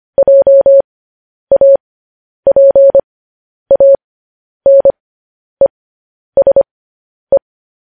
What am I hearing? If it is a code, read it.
Morse code, heard as a keyed tone, about 25 wpm, spaced as for 10 wpm, 550 Hz. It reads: JAPANESE
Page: news-headlines-in-morse-code-at-10-wpm